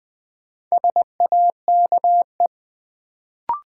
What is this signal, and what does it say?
Morse code, heard as a keyed tone, about 21 wpm, 700 Hz. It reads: SAKE
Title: Sets of 4 Words - New Word Drill - 20wpm